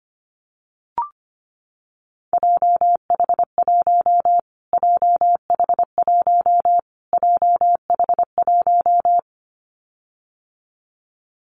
Morse code, heard as a keyed tone, about 25 wpm, 700 Hz. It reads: JH1 JH1 JH1
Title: Callsign Prefixes for Top 25 DXCC - Repeated 3x - 25wpm